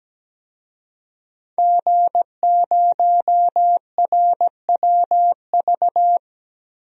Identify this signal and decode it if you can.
Morse code, heard as a keyed tone, about 17 wpm, 700 Hz. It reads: G0RWV
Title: DXCC-100 Call Signs - 5 Characters - 17wpm